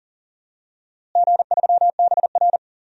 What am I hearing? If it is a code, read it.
Morse code, heard as a keyed tone, about 37 wpm, 700 Hz. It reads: G3BR
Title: DXCC-100 Call Signs - 4 Characters - Repeated 2x - 40wpm